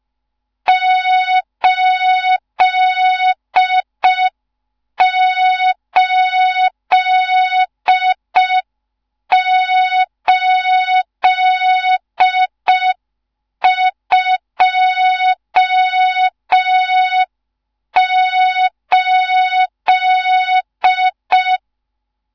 Morse code, heard as a keyed tone, about 5 wpm, 3700 Hz. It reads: 88828
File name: Morze-HA2MN / hang / 14 280 HA2MN